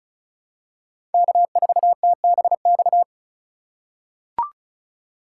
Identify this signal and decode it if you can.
Morse code, heard as a keyed tone, about 36 wpm, 700 Hz. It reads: K4TBX
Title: USA Call Signs - 5 Characters - 3x Character Spacing - 35wpm